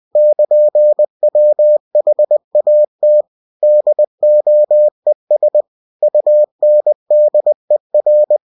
Morse code, heard as a keyed tone, about 20 wpm, 600 Hz. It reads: <KN>WHAT DOES UNDER
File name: cw-books / winnie-the-pooh / winnie-20wpm-page-0002